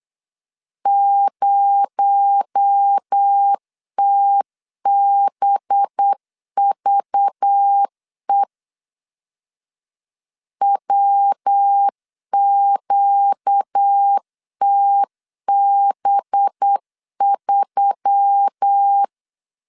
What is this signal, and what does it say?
Morse code, heard as a keyed tone, about 8 wpm, 800 Hz. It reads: 0TBVE WQTB3
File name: LEZIONI / 6 PKX0 / 6 PKX0 8wpm 3341